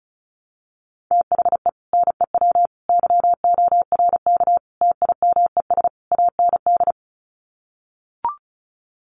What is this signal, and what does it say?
Morse code, heard as a keyed tone, about 34 wpm, 700 Hz. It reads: THE NEW YORK TIMES AND
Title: Sets of 5 Words Encore 35wpm